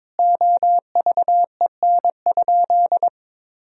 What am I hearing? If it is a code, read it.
Morse code, heard as a keyed tone, about 22 wpm, 700 Hz. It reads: OVEN?
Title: Sentences from Top 1000 Words - Review Part 4 - 22wpm